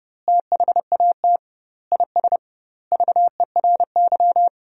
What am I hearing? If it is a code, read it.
Morse code, heard as a keyed tone, about 30 wpm, 700 Hz. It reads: THAT IS VERY
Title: 2x Word-Spacing - Sentences from Top 500 Words - 30wpm